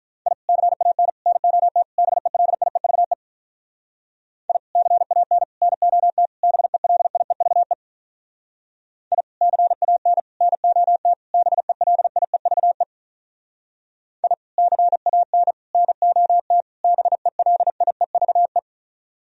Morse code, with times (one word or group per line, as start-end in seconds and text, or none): 0.3-0.3 I
0.5-1.1 CAN
1.3-1.8 NOT
2.0-3.1 BELIEVE
4.5-4.6 I
4.8-5.4 CAN
5.6-6.3 NOT
6.4-7.7 BELIEVE
9.1-9.2 I
9.4-10.2 CAN
10.4-11.1 NOT
11.3-12.8 BELIEVE
14.2-14.3 I
14.6-15.5 CAN
15.7-16.6 NOT
16.8-18.6 BELIEVE